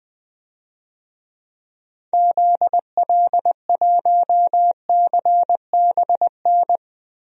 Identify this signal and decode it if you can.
Morse code, heard as a keyed tone, about 20 wpm, 700 Hz. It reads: ZL1CBN
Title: DXCC-100 Call Signs - 6 Characters - Repeated 3x - 20wpm